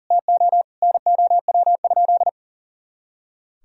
Morse code, heard as a keyed tone, about 37 wpm, 700 Hz. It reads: TO NOW?
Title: Sentences from Top 200 Words and Abbreviations 40wpm